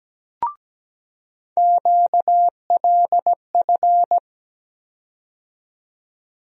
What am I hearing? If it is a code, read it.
Morse code, heard as a keyed tone, about 17 wpm, 700 Hz. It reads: QLF
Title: USA Call Sign Postfixes - 3 Letters 17wpm